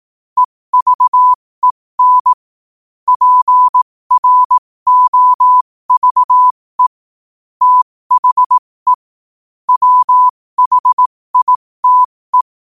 Morse code, heard as a keyed tone, about 18 wpm, spaced as for 15 wpm, 1000 Hz. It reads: EVEN PROVE THE WHITE